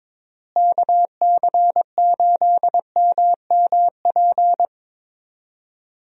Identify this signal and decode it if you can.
Morse code, heard as a keyed tone, about 21 wpm, 700 Hz. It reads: KC8MMP
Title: USA Call Signs - 6 Characters - Repeated 3x - 22wpm